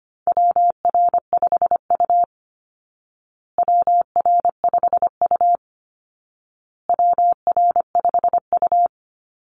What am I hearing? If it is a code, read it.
Morse code, heard as a keyed tone, about 25 wpm, 700 Hz. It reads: WR5U WR5U WR5U
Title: Call Signs Repeated 2x - 25wpm